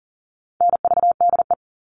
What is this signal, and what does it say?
Morse code, heard as a keyed tone, about 40 wpm, 700 Hz. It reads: NVDE
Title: All Cut Numbers - 4-Digit 40wpm